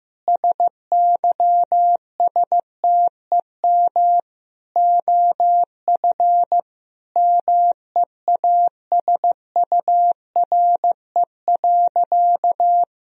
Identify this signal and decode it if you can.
Morse code, heard as a keyed tone, about 15 wpm, 700 Hz. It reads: SYSTEM OF MEASURE.
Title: Sentences from Top 1000 Words - Review Part 3 - 15wpm